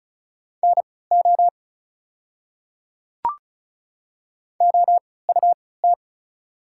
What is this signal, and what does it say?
Morse code, heard as a keyed tone, about 36 wpm, spaced as for 20 wpm, 690 Hz. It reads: NO OUT
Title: 3x Character Spacing - Top 100 Words - 35wpm